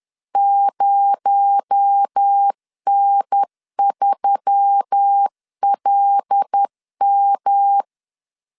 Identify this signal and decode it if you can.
Morse code, heard as a keyed tone, about 10 wpm, 800 Hz. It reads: 0N3LM
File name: LEZIONI / 6 PKX0 / 6 PKX0 10wpm 2707